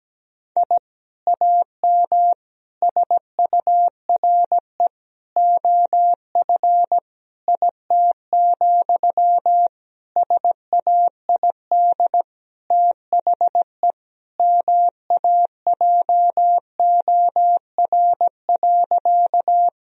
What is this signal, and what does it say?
Morse code, heard as a keyed tone, about 17 wpm, 700 Hz. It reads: I AM SURE OF IT, SAID THE MAJOR.
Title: Sentences from Top 1000 Words - Review Part 4 - 17wpm